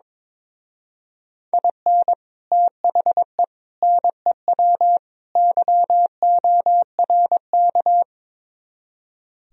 Morse code, heard as a keyed tone, about 24 wpm, 700 Hz. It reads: IN THE NEW YORK
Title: Speed-Racing - Sets of 4 Words 22wpm